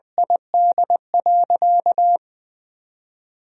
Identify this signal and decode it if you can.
Morse code, heard as a keyed tone, about 20 wpm, 700 Hz. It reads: ID.
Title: 3x Word-Spacing - Sentences from Top 200 Words - 20wpm